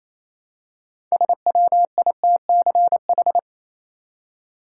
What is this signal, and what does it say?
Morse code, heard as a keyed tone, about 28 wpm, 700 Hz. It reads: SWITCH
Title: Six Letter Words - 28wpm